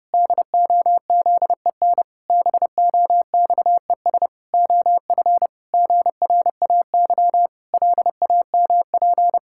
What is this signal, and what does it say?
Morse code, heard as a keyed tone, about 28 wpm, 700 Hz. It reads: DOZEN BOXES OF GRAY LAMP
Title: Pangram Challenge v2 - 30wpm